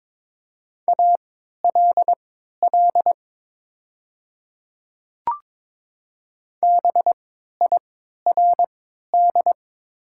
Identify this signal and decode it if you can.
Morse code, heard as a keyed tone, about 22 wpm, spaced as for 12 wpm, 690 Hz. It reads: ALL BIRD
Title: 3x Character Spacing - Top 500 Words - 22wpm